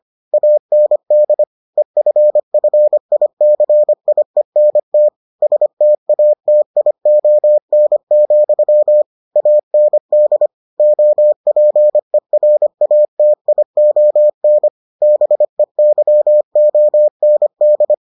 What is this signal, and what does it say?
Morse code, heard as a keyed tone, about 25 wpm, 600 Hz. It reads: AND EFFICIENT STATION, AND OPERATION BEYOND